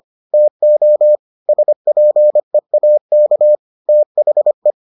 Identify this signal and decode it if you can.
Morse code, heard as a keyed tone, about 25 wpm, 600 Hz. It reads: TO SPEAK THE